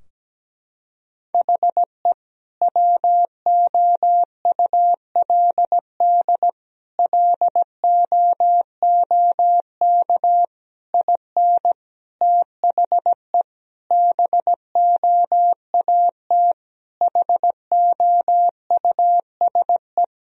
Morse code, heard as a keyed tone, about 17 wpm, 700 Hz. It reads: HE WOULD LOOK IN THE BOAT HOUSE
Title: Sentences from Top 1000 Words - Review Part 3 - 17wpm